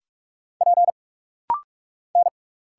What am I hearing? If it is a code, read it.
Morse code, heard as a keyed tone, about 46 wpm, 700 Hz. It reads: P N
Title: Single Letter-Number Alt - Rapid-Fire 45wpm